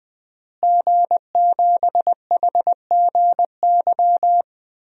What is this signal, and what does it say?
Morse code, heard as a keyed tone, about 21 wpm, 700 Hz. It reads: G7HGY
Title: DXCC-100 Call Signs - 5 Characters - Repeated 2x - 20wpm